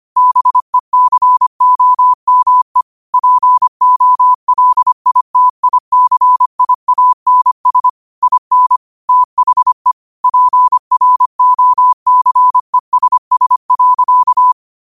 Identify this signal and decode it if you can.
Morse code, heard as a keyed tone, about 25 wpm, 1000 Hz. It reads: BECOME POLITICIANS IN THE PROCESS.
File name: qotdfiles / 25wpm